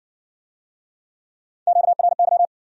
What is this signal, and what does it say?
Morse code, heard as a keyed tone, about 59 wpm, 700 Hz. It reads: QRY